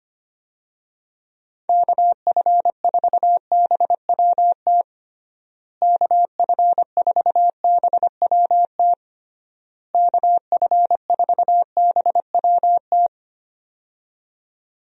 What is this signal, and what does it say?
Morse code, heard as a keyed tone, about 25 wpm, 700 Hz. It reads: KF4BWT KF4BWT KF4BWT
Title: USA Call Signs - 6 Characters - Repeated 3x - 25wpm